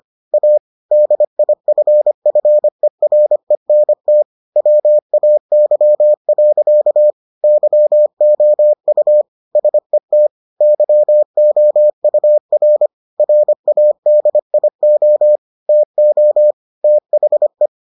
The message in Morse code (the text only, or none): A DIFFERENT WAY. YOU SET YOUR RADIO TO THE